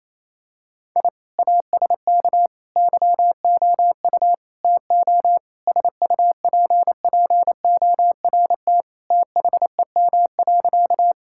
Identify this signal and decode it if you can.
Morse code, heard as a keyed tone, about 28 wpm, 700 Hz. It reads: I ASK YOU TO SUPPORT THEM.
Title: Sentences from Top 1000 Words - Review Part 5 - 28wpm